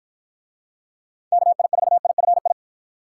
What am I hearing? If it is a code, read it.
Morse code, heard as a keyed tone, about 46 wpm, 700 Hz. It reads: KI4IFI